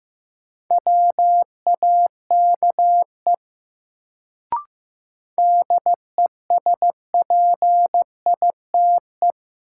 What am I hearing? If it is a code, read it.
Morse code, heard as a keyed tone, about 15 wpm, 700 Hz. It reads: WAKE DESPITE
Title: Sets of 4 Words - New Word Drill - 15wpm